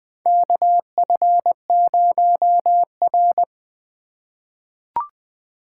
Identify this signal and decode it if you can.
Morse code, heard as a keyed tone, about 20 wpm, 700 Hz. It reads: KF0R
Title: USA Call Signs - 4 Characters - 3x Character Spacing - 20wpm